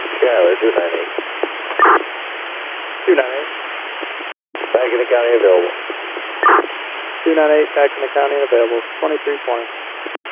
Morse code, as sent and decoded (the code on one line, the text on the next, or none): none